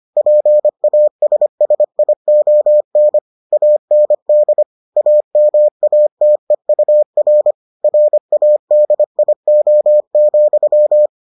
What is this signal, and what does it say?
Morse code, heard as a keyed tone, about 25 wpm, 600 Hz. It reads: PASSION AND AMATEUR RADIO,